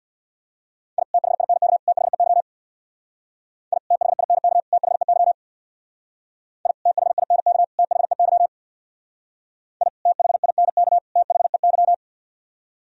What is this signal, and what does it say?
Morse code, heard as a keyed tone, about 60 wpm, 700 Hz. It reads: I THINK THEY I THINK THEY I THINK THEY I THINK THEY
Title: Speed-Racing - Sets of 3 Words 50wpm